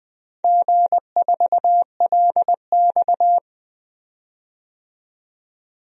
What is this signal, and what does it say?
Morse code, heard as a keyed tone, about 20 wpm, 700 Hz. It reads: G4LX
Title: DXCC-100 Call Signs - 4 Characters - 20wpm